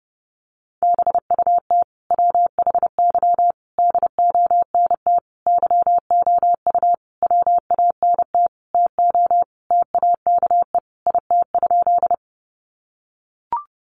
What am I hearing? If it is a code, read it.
Morse code, heard as a keyed tone, about 30 wpm, 700 Hz. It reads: BUT WHY DONT YOU WANT TO TAKE IT?
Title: Sentences from Top 300 Words 30wpm